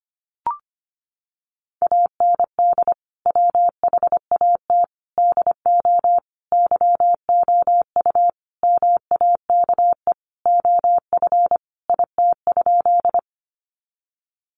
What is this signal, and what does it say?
Morse code, heard as a keyed tone, about 25 wpm, 700 Hz. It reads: AND WHAT DO YOU MAKE OF IT?
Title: Sentences from Top 100 Words 25wpm